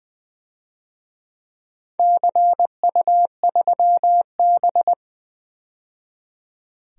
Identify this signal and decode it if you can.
Morse code, heard as a keyed tone, about 20 wpm, 700 Hz. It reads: CU3B
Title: DXCC-100 Call Signs - 4 Characters - 20wpm